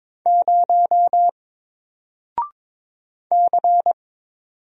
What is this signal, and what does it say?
Morse code, heard as a keyed tone, about 22 wpm, 700 Hz. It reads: 0 C
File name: Single-Letter-Number-22wpm